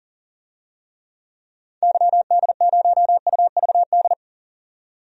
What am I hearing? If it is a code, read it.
Morse code, heard as a keyed tone, about 40 wpm, 700 Hz. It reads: YD0UVD
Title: DXCC-100 Call Signs - 6 Characters - 40wpm